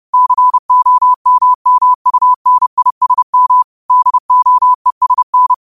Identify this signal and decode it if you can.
Morse code, heard as a keyed tone, about 30 wpm, 1000 Hz. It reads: COMMUNISM DOESN